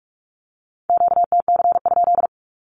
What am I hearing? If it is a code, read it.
Morse code, heard as a keyed tone, about 46 wpm, 700 Hz. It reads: QTC?
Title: Q-Codes with Spoken Meaning - 45wpm